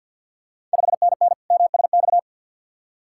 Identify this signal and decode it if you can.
Morse code, heard as a keyed tone, about 51 wpm, 690 Hz. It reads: 5NN MSX
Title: Ontario QSO Party Exchange 50wpm